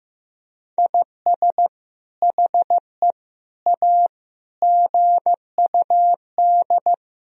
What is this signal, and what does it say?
Morse code, heard as a keyed tone, about 15 wpm, 700 Hz. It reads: IS HE A GUD
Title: Sentences from Top 200 Words and Abbreviations 15wpm